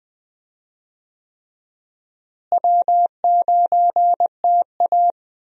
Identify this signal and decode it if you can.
Morse code, heard as a keyed tone, about 20 wpm, 700 Hz. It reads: W9TA
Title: USA Call Signs - 4 Characters - Repeated 3x - 20wpm